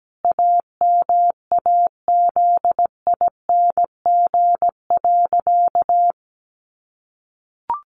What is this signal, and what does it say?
Morse code, heard as a keyed tone, about 17 wpm, 700 Hz. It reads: AMAZING.
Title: Sentences from Top 1500 Words 17wpm